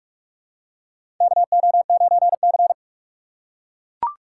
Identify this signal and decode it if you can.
Morse code, heard as a keyed tone, about 41 wpm, 700 Hz. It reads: KO9C
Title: USA Call Signs - 4 Characters - 6x Character Spacing - 45wpm